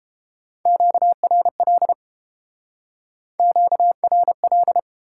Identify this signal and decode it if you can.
Morse code, heard as a keyed tone, about 30 wpm, 700 Hz. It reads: QRL QRL